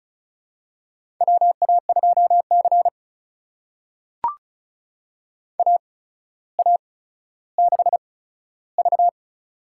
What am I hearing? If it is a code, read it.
Morse code, heard as a keyed tone, about 36 wpm, spaced as for 15 wpm, 700 Hz. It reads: WA2C AA6V
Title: USA Call Signs - 4 Characters - 8x Character Spacing - 35wpm